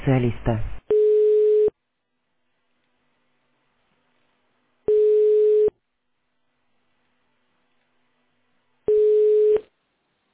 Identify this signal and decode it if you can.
Ringback tone (425 Hz, 1 s on, 3 s off), with speech.